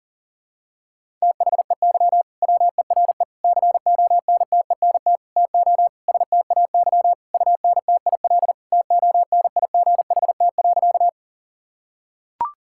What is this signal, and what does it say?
Morse code, heard as a keyed tone, about 40 wpm, 700 Hz. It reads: THEY WERE CONTENT TO STAY UNTIL TONIGHT.
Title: Sentences from Top 1000 Words - Review Part 6 - 40wpm